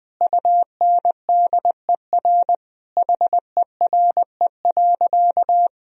Morse code, heard as a keyed tone, about 21 wpm, 700 Hz. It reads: UNDER HERE.
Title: Sentences from Top 1000 Words - Review Part 1 - 20wpm